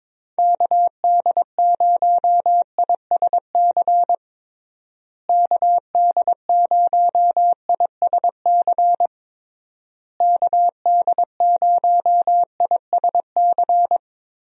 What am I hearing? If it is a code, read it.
Morse code, heard as a keyed tone, about 22 wpm, 700 Hz. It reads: KD0ISC KD0ISC KD0ISC
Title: USA Call Signs - 6 Characters - Repeated 3x - 22wpm